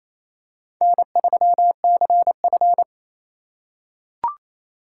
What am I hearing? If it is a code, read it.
Morse code, heard as a keyed tone, about 28 wpm, 700 Hz. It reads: N3CF
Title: USA Call Signs - 4 Characters - 8x Character Spacing - 28wpm